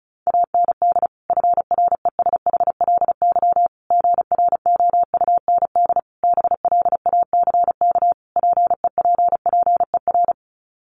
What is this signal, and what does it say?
Morse code, heard as a keyed tone, about 34 wpm, 700 Hz. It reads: AND FRESHLY GROUND BLACK PEPPER
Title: Sets of 5 Words Encore 35wpm